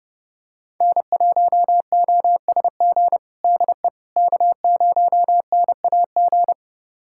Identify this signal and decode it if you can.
Morse code, heard as a keyed tone, about 32 wpm, 700 Hz. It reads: N1OSG DE K0NAG
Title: USA Callsign DE Callsign - 5 Characters - 30wpm